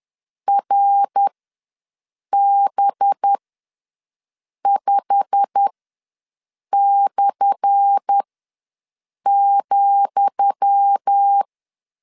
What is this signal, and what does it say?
Morse code, heard as a keyed tone, about 10 wpm, spaced as for 6 wpm, 800 Hz. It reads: RB5/,
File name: LEZIONI / 10 SIGNS / 10D COMA ws 10wpm 4314